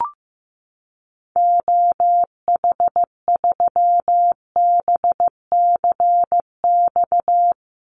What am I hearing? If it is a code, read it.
Morse code, heard as a keyed tone, about 15 wpm, 700 Hz. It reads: OH3BCX
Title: Call Signs Repeated 3x - 15wpm